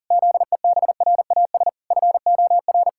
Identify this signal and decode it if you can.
Morse code, heard as a keyed tone, about 40 wpm, 700 Hz. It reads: ZEBRAS FOR
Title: Pangram Challenge v2 - 40wpm